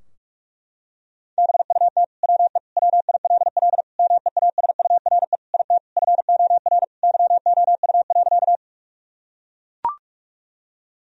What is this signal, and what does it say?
Morse code, heard as a keyed tone, about 42 wpm, 700 Hz. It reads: BUT WE WILL MEASURE IT FOR YOU.
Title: Sentences from Top 1000 Words - Review Part 2 - 45wpm